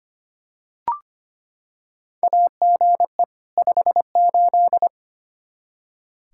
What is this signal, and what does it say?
Morse code, heard as a keyed tone, about 25 wpm, 700 Hz. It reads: AGE 58
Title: Practice QSO - Exchange 3 - More Info - 25wpm